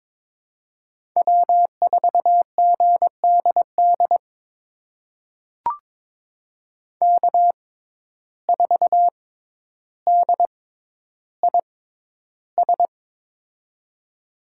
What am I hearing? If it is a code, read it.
Morse code, heard as a keyed tone, about 21 wpm, spaced as for 11 wpm, 700 Hz. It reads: W4GDD K4DIS
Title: USA Call Signs - 5 Characters - 6x Character Spacing - 22wpm